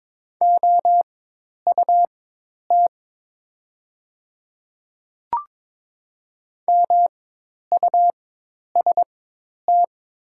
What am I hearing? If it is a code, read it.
Morse code, heard as a keyed tone, about 22 wpm, spaced as for 10 wpm, 700 Hz. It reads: OUT MUST
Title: 4x Character Spacing - Top 200 Words - 22wpm